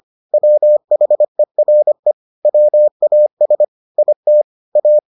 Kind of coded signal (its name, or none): Morse code